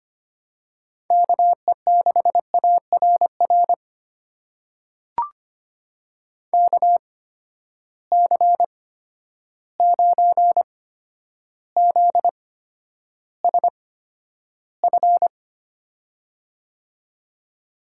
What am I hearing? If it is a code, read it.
Morse code, heard as a keyed tone, about 25 wpm, spaced as for 11 wpm, 700 Hz. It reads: KE6ARR KC9ZSF
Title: USA Call Signs - 6 Characters - 8x Character Spacing - 25wpm